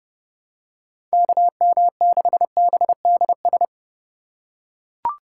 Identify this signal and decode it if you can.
Morse code, heard as a keyed tone, about 28 wpm, 700 Hz. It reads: KM6BDS